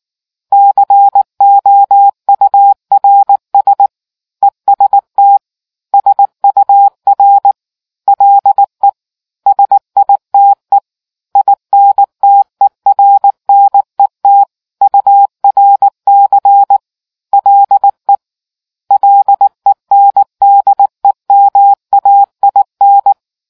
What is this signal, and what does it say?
Morse code, heard as a keyed tone, about 19 wpm, 800 Hz. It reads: COURS EST SUR LE SITE INTERNET URC LE LENDEMAIN